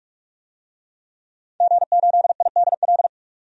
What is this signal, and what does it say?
Morse code, heard as a keyed tone, about 46 wpm, 700 Hz. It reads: G8IDL
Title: DXCC-100 Call Signs - 5 Characters - 45wpm